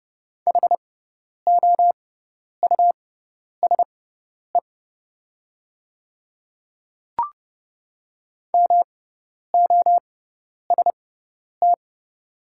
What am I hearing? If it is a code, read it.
Morse code, heard as a keyed tone, about 30 wpm, spaced as for 10 wpm, 690 Hz. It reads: HOUSE MOST
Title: 6x Character Spacing - Top 200 Words - 30wpm